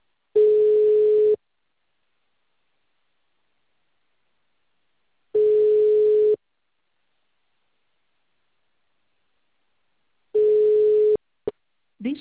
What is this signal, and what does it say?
Ringback tone (Europe, 425 Hz), with speech.